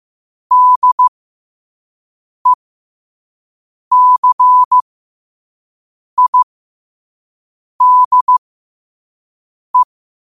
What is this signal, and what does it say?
Morse code, heard as a keyed tone, about 15 wpm, spaced as for 5 wpm, 1000 Hz. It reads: DECIDE
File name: qotdfiles / 5wpm